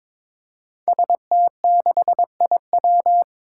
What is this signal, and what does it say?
Morse code, heard as a keyed tone, about 21 wpm, 700 Hz. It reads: ST6IW
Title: Five Challenging Letter-Number Combinations 22wpm